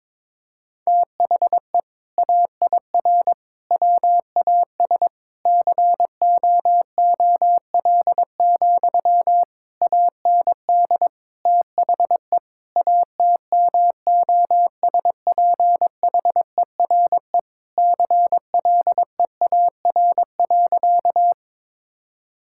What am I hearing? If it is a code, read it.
Morse code, heard as a keyed tone, about 21 wpm, 700 Hz. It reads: THE AIR WAS COOL, AND THE ATMOSPHERE CLEAR.
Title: Sentences from Top 1600 Words 22wpm